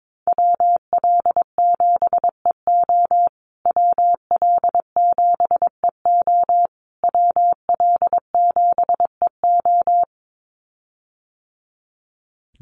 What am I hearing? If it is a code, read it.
Morse code, heard as a keyed tone, about 22 wpm, 700 Hz. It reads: WL7EO WL7EO WL7EO
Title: USA Call Signs Repeated 3x 22wpm